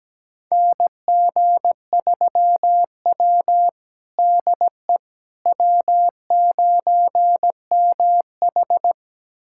Morse code, heard as a keyed tone, about 17 wpm, 700 Hz. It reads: NG3W DE W9MH
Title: USA Callsign DE Callsign - 4 Characters - 17wpm